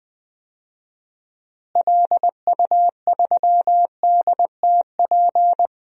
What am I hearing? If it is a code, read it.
Morse code, heard as a keyed tone, about 20 wpm, 700 Hz. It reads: LU3DTP
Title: DXCC-100 Call Signs - 6 Characters - Repeated 3x - 20wpm